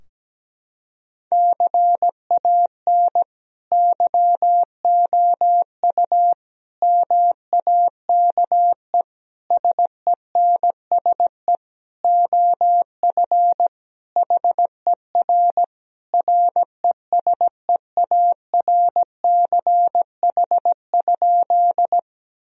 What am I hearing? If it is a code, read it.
Morse code, heard as a keyed tone, about 17 wpm, 700 Hz. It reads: CAN YOU MAKE SENSE OF HER RESEARCH?
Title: Sentences from Top 1000 Words - Review Part 4 - 17wpm